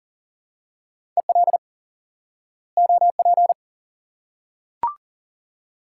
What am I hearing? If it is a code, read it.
Morse code, heard as a keyed tone, about 40 wpm, 700 Hz. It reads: EL OP
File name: IQE Course 3 - Lesson 89 - YAGI - Two Elements - 40wpm